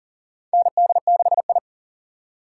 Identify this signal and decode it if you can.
Morse code, heard as a keyed tone, about 40 wpm, 700 Hz. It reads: ND6I
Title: USA Call Signs - 4 Characters - 3x Character Spacing - 40wpm